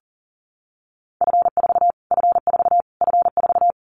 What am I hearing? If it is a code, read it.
Morse code, heard as a keyed tone, about 40 wpm, 700 Hz. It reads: F4 F4 F4
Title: Callsign Prefixes for Top 25 DXCC - Repeated 3x - 40wpm